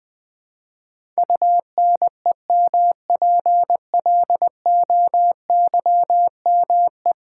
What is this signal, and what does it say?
Morse code, heard as a keyed tone, about 20 wpm, 700 Hz. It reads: UNEMPLOYME
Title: Build A Word - 5 Letters At A Time - 12-Letter Words - 20wpm